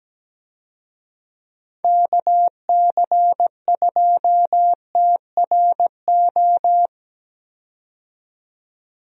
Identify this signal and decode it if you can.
Morse code, heard as a keyed tone, about 17 wpm, 700 Hz. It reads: KC2TRO
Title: USA Call Signs - 6 Characters - 17wpm